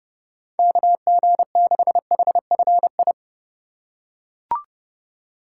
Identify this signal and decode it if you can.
Morse code, heard as a keyed tone, about 32 wpm, 700 Hz. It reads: KG6HFI
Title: USA Call Signs - 6 Characters - 2x Character Spacing - 30wpm